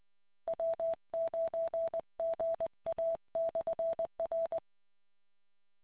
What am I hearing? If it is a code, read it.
Morse code, heard as a keyed tone, about 20 wpm, 660 Hz. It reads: W9GA/R